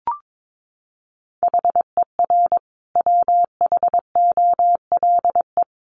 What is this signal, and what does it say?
Morse code, heard as a keyed tone, about 22 wpm, 700 Hz. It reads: HER WHOLE
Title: Sentences from Top 1500 Words 22wpm